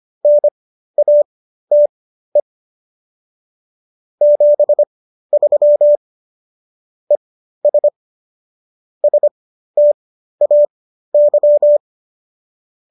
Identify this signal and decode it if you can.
Morse code, heard as a keyed tone, about 25 wpm, spaced as for 13 wpm, 590 Hz. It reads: NATE 73 ES STAY